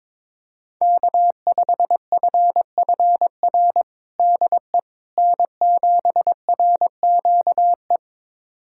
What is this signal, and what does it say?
Morse code, heard as a keyed tone, about 22 wpm, 700 Hz. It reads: K5FFR DE N7RQE
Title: USA Callsign DE Callsign - 5 Characters - 22wpm